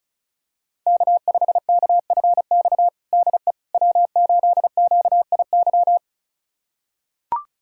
Morse code, heard as a keyed tone, about 36 wpm, 690 Hz. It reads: K5KFX DE W8QIY
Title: USA Callsign DE Callsign - 5 Characters - 35wpm